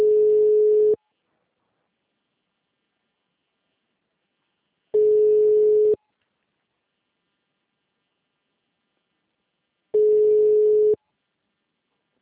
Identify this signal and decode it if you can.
Ringback tone (Europe, 425 Hz).